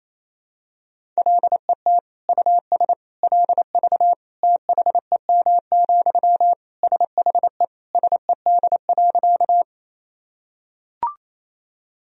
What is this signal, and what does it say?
Morse code, heard as a keyed tone, about 28 wpm, 700 Hz. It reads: LET US LV THEM, SHE SED.
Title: Sentences from Top 400 Words and Abbreviations 28wpm